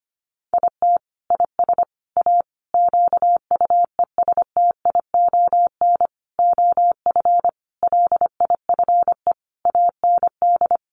Morse code, heard as a keyed tone, about 25 wpm, 690 Hz. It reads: IT IS A QUESTION OF LIFE AND
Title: Sentences from Top 1100 Words 25wpm